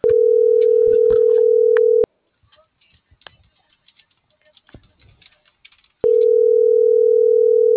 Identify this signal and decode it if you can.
Ringback tone (North America), 440 + 480 Hz.